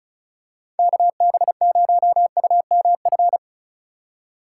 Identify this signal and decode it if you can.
Morse code, heard as a keyed tone, about 36 wpm, 700 Hz. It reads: KB0UMF
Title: USA Call Signs - 6 Characters - 3x Character Spacing - 35wpm